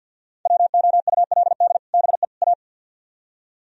Morse code, heard as a keyed tone, about 60 wpm, 700 Hz. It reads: WOULD BE A
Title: Speed-Racing - Sets of 3 Words 50wpm